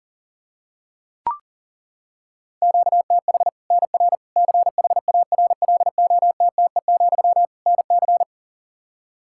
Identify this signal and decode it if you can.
Morse code, heard as a keyed tone, about 40 wpm, 700 Hz. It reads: QTH NR CHARLOTTE, NC
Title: Practice QSO - Exchange 1 - Beginning - 40wpm